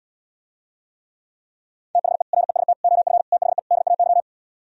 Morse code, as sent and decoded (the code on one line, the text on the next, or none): - .... / .-. ... - / --- ..-. / - .... / -.. .- -.--
THE REST OF THE DAY